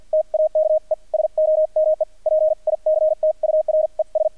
Morse code, heard as a keyed tone, about 48 wpm, spaced as for 41 wpm, 620 Hz, with radio noise. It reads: TAKE SOME PICTURES
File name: archived / CRSnet